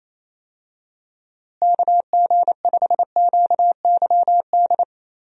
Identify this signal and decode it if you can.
Morse code, heard as a keyed tone, about 28 wpm, 700 Hz. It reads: KG5QYD
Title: USA Call Signs - 6 Characters - Repeated 3x - 28wpm